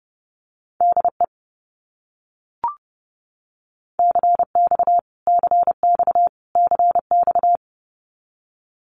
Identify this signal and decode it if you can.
Morse code, heard as a keyed tone, about 30 wpm, 700 Hz. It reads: DE CX CX CX
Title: Abbreviations Repeated 3x - 30wpm